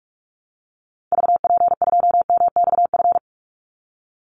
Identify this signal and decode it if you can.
Morse code, heard as a keyed tone, about 44 wpm, 700 Hz. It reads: VP2MXF
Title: Call Signs Repeated 3x - 45wpm